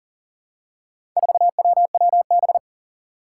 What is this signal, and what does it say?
Morse code, heard as a keyed tone, about 40 wpm, 700 Hz. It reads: 4WWB
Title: Four Challenging Letter-Number Combinations 40wpm